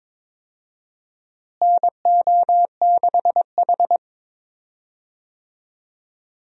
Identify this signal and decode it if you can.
Morse code, heard as a keyed tone, about 22 wpm, 700 Hz. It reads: NO6H
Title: USA Call Signs - 4 Characters - Repeated 2x - 22wpm